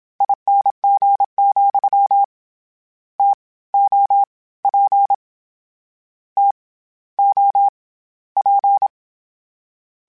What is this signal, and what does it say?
Morse code, heard as a keyed tone, about 26 wpm, 800 Hz. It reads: ING, T O P T O P
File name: Variable Spacing / Variable Spacing-mp3s / TOs @25